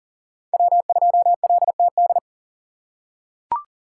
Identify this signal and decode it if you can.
Morse code, heard as a keyed tone, about 40 wpm, 700 Hz. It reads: W2LTD
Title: USA Call Signs - 5 Characters - 3x Character Spacing - 40wpm